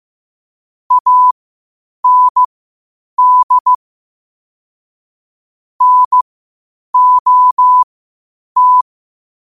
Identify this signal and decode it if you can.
Morse code, heard as a keyed tone, about 15 wpm, spaced as for 8 wpm, 1000 Hz. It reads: AND NOT
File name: qotdfiles / 8wpm